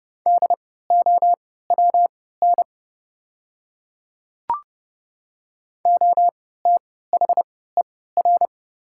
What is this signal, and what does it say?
Morse code, heard as a keyed tone, about 30 wpm, spaced as for 17 wpm, 700 Hz. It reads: DOWN OTHER
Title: 3x Character Spacing - Top 100 Words - 30wpm